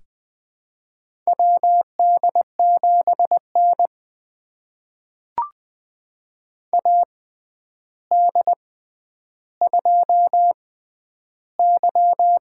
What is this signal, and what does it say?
Morse code, heard as a keyed tone, about 20 wpm, spaced as for 10 wpm, 700 Hz. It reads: WD7N AD2Y
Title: USA Call Signs - 4 Characters - 6x Character Spacing - 20wpm